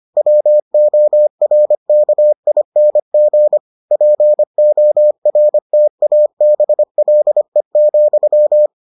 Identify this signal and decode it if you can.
Morse code, heard as a keyed tone, about 25 wpm, 600 Hz. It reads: WORKING PORTABLE,